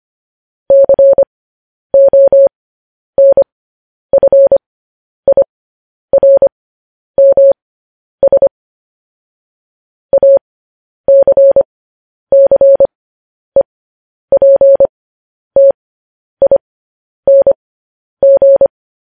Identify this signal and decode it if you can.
Morse code, heard as a keyed tone, about 25 wpm, spaced as for 10 wpm, 550 Hz. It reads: CONFIRMS ACCEPTING
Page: news-headlines-in-morse-code-at-10-wpm